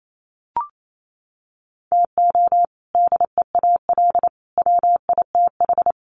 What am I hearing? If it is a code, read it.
Morse code, heard as a keyed tone, about 28 wpm, 700 Hz. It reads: TO DEAL WITH
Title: Sets of 3 Words 28wpm